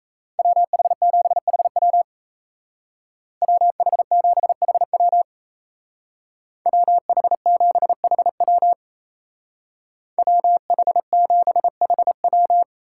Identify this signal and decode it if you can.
Morse code, heard as a keyed tone, about 34 wpm, 700 Hz. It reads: WH7HW WH7HW WH7HW WH7HW
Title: Speed Racing - USA 2x2 Call Signs 28wpm